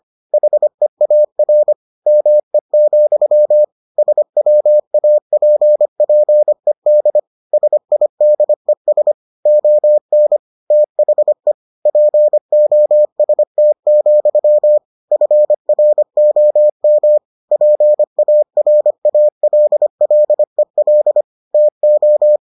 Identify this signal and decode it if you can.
Morse code, heard as a keyed tone, about 25 wpm, 600 Hz. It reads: HEAR ME, SWAPPED SIDES ON THE POST, FROM PARALLEL TO